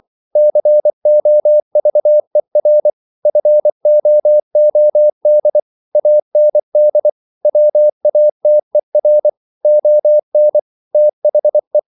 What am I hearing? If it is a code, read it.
Morse code, heard as a keyed tone, about 24 wpm, 600 Hz. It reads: COVER FOOD AND WATER ON THE